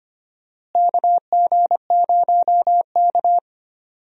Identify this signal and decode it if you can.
Morse code, heard as a keyed tone, about 25 wpm, 700 Hz. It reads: KG0K
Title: USA Call Signs - 4 Characters - Repeated 2x - 25wpm